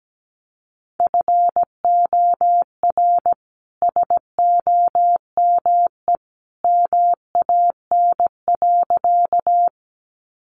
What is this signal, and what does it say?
Morse code, heard as a keyed tone, about 17 wpm, 700 Hz. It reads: FOR SOME MAN.
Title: Sentences from Top 200 Words 17wpm